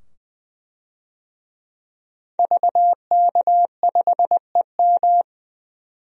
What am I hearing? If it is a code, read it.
Morse code, heard as a keyed tone, about 20 wpm, 700 Hz. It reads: VK5EM
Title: DXCC-100 Call Signs - 5 Characters - 20wpm